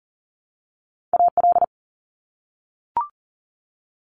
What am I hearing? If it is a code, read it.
Morse code, heard as a keyed tone, about 40 wpm, 700 Hz. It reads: AL
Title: US States Repeated 3x 40wpm